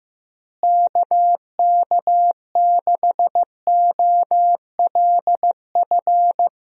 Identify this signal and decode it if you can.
Morse code, heard as a keyed tone, about 15 wpm, 700 Hz. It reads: KK6OLF
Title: USA Call Signs - 6 Characters - Repeated 3x - 15wpm